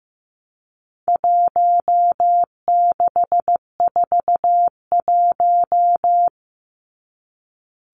Morse code, heard as a keyed tone, about 15 wpm, 700 Hz. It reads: 1641
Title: Numbers - Contest Style - 4-Digit 15wpm